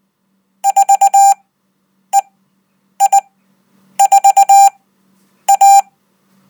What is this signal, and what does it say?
Morse code, heard as a keyed tone, about 20 wpm, spaced as for 9 wpm, 790 Hz. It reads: 4EI4A
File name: Code